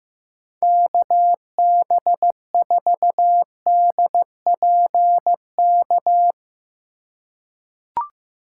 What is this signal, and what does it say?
Morse code, heard as a keyed tone, about 15 wpm, 700 Hz. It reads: KB4DPK
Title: USA Call Signs - 4x Character Spacing - 15wpm